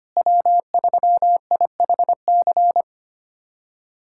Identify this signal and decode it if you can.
Morse code, heard as a keyed tone, about 25 wpm, 700 Hz. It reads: W3IHC
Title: USA Callsign DE Callsign 25wpm